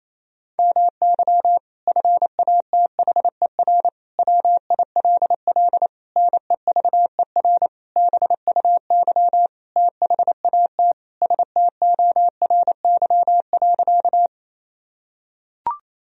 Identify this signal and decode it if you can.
Morse code, heard as a keyed tone, about 28 wpm, 700 Hz. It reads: MY FATHER WILL NEVER BUY THAT STORY.
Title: Sentences from Top 1000 Words - Review Part 4 - 28wpm